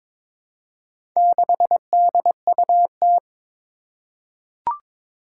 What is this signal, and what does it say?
Morse code, heard as a keyed tone, about 23 wpm, 700 Hz. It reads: 6DUT
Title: Four Challenging Letter-Number Combinations 22wpm